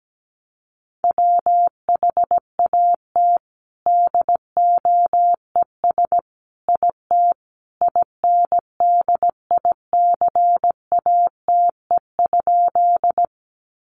Morse code, heard as a keyed tone, about 17 wpm, 700 Hz. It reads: WHAT DOES IT INDICATE?
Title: Sentences from Top 1300 Words 17wpm